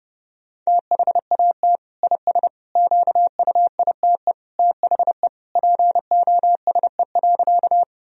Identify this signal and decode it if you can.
Morse code, heard as a keyed tone, about 30 wpm, 700 Hz. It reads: THAT IS QUITE THE POSE.
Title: Sentences from Top 1000 Words - Review Part 2 - 30wpm